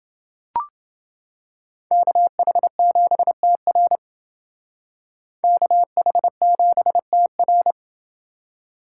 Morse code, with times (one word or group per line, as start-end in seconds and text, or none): 1.9-4.0 KH7TR
5.4-7.7 KH7TR